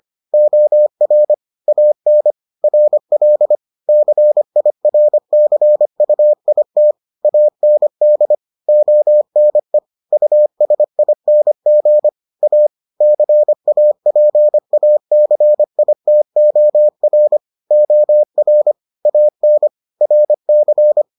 Morse code, heard as a keyed tone, about 25 wpm, 600 Hz. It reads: OR AN RL CIRCUIT AND ONE USING A CAPACITOR OR AN RC